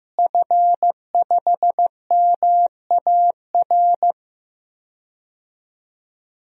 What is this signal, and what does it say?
Morse code, heard as a keyed tone, about 15 wpm, 700 Hz. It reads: F5MAR
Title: DXCC-100 Call Signs - 5 Characters - Repeated 2x - 15wpm